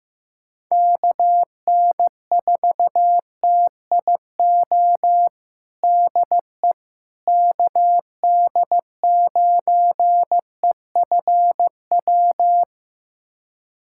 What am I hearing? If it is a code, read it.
Morse code, heard as a keyed tone, about 15 wpm, 700 Hz. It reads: KN4TIO DE KD9EFW
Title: USA Callsign DE Callsign - 6 Characters - 15wpm